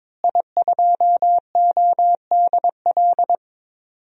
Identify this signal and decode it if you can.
Morse code, heard as a keyed tone, about 22 wpm, 700 Hz. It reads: I2ODL
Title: DXCC-100 Call Signs - 5 Characters - Repeated 3x - 22wpm